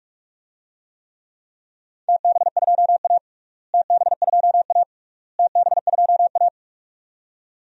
Morse code, heard as a keyed tone, about 42 wpm, 700 Hz. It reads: TB2A TB2A TB2A
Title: DXCC-100 Call Signs - 4 Characters - Repeated 3x - 45wpm